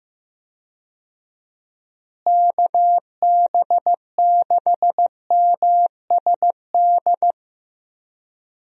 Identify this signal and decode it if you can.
Morse code, heard as a keyed tone, about 15 wpm, 700 Hz. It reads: KB6MSD
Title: USA Call Signs - 6 Characters - Repeated 3x - 15wpm